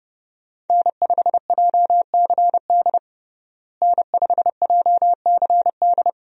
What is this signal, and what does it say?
Morse code, heard as a keyed tone, about 30 wpm, 700 Hz. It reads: N5JCD N5JCD
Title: USA Call Signs - 5 Characters - Repeated 2x - 30wpm